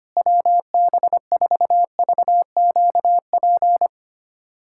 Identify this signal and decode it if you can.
Morse code, heard as a keyed tone, about 25 wpm, 700 Hz. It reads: WB4VQP
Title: USA Callsign DE Callsign - 6 Characters - 25wpm